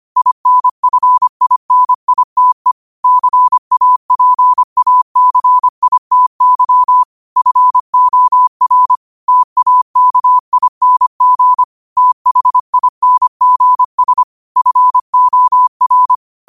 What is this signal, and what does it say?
Morse code, heard as a keyed tone, about 25 wpm, 1000 Hz. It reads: INFINITE CAPACITY FOR TAKING THINGS FOR
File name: qotdfiles / 25wpm